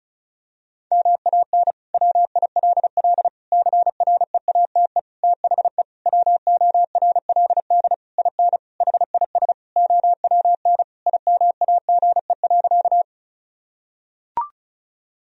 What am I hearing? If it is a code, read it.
Morse code, heard as a keyed tone, about 36 wpm, 700 Hz. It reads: MAN WILL CREATE THE WORLD IN HIS OWN IMAGE.
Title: Sentences from Top 1000 Words - Review Part 4 - 35wpm